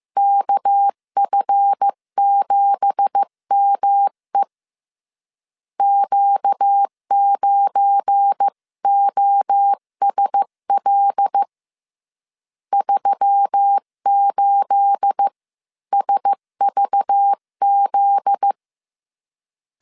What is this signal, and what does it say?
Morse code, heard as a keyed tone, about 15 wpm, 800 Hz. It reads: KF7ME Q9OSL 38SVZ
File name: CW / DET / 1 9 1500C S B 14wpm 3302